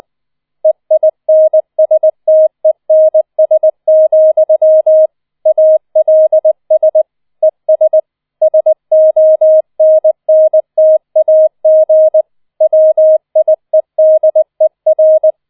Morse code, heard as a keyed tone, about 20 wpm, 630 Hz. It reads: EINSTENS, ALS ES SONNTAG WIEDER